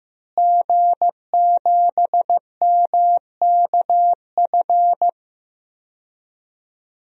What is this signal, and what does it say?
Morse code, heard as a keyed tone, about 15 wpm, 700 Hz. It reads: G7MKF
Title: DXCC-100 Call Signs - 5 Characters - Repeated 2x - 15wpm